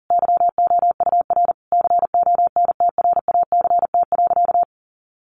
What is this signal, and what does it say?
Morse code, heard as a keyed tone, about 40 wpm, 700 Hz. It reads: YOUR CONTRACT.
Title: Sentences from Top 1400 Words 40wpm